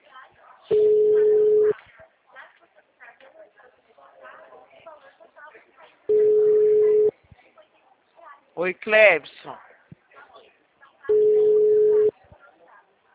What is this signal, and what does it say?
Ringback tone (Europe, 425 Hz), with speech.